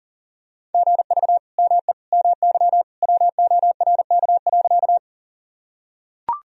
Morse code, heard as a keyed tone, about 40 wpm, 700 Hz. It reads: GV ME MY WORK.
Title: Sentences from Top 200 Words and Abbreviations 40wpm